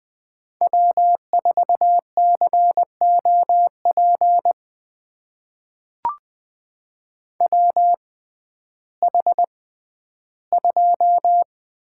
Morse code, heard as a keyed tone, about 20 wpm, spaced as for 12 wpm, 700 Hz. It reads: W4COP WH2
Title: USA Call Signs - 6x Character Spacing - 20wpm